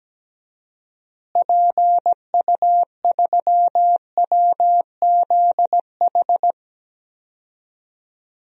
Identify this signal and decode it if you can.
Morse code, heard as a keyed tone, about 17 wpm, 700 Hz. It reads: PU3WZH
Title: DXCC-100 Call Signs - 6 Characters - Repeated 2x - 17wpm